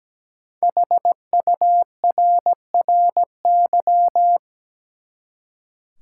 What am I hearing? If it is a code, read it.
Morse code, heard as a keyed tone, about 17 wpm, 700 Hz. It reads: HURRY
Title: Five Letter Words - 17wpm